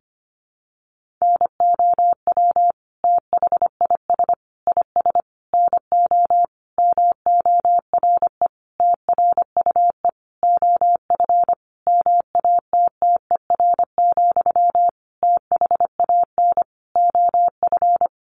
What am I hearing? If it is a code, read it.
Morse code, heard as a keyed tone, about 25 wpm, 700 Hz. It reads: NOW THIS IS NO MORE TRUE OF MATTER, THAN OF
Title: Sentences from Top 1400 Words 25wpm